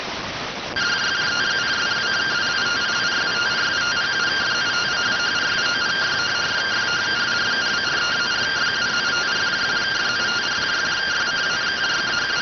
Two-tone teleprinter signal, about 170 Hz shift.